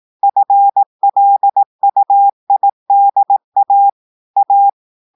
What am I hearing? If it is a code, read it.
Morse code, heard as a keyed tone, about 18 wpm, 800 Hz. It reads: FLUIDA A
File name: audios / portugais / portugais texte en clair 1 vitesse 18 mots minutes